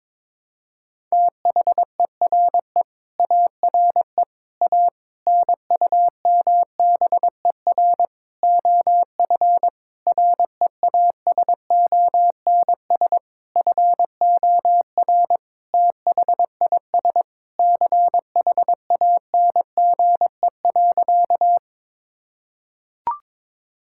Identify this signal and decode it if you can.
Morse code, heard as a keyed tone, about 21 wpm, 700 Hz. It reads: THERE ARE A NUMBER OF REASONS FOR THIS CHANGE.
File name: practice / 22wpm / Sentences with 8-Word Phrases - 22wpm